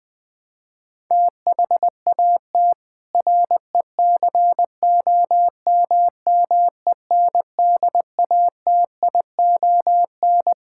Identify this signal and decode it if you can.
Morse code, heard as a keyed tone, about 20 wpm, 700 Hz. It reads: THAT RECOMMENDATION
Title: Sentences from Top 1600 Words 20wpm